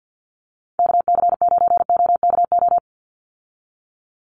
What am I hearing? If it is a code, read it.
Morse code, heard as a keyed tone, about 48 wpm, 700 Hz. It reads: KC9OKO
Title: USA Call Signs 50wpm